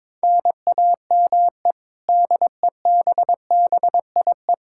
Morse code, heard as a keyed tone, about 21 wpm, 700 Hz. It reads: NAME DEBBIE
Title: Practice QSO - Exchange 1 - Beginning - 22wpm